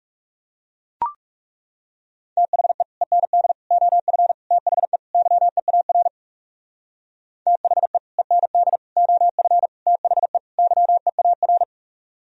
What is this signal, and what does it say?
Morse code, heard as a keyed tone, about 37 wpm, 700 Hz. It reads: THE END OF THE YEAR THE END OF THE YEAR
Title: Speed-Racing - Sets of 5 Words 30wpm